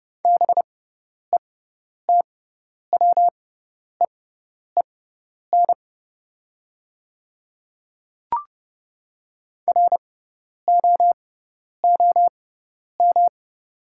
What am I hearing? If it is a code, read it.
Morse code, heard as a keyed tone, about 31 wpm, spaced as for 10 wpm, 700 Hz. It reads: BETWEEN ROOM
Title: 6x Character Spacing - Top 300 Words - 30wpm